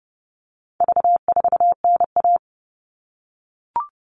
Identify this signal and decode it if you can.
Morse code, heard as a keyed tone, about 30 wpm, 700 Hz. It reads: V4NA
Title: All Cut Numbers - 4-Digit 30wpm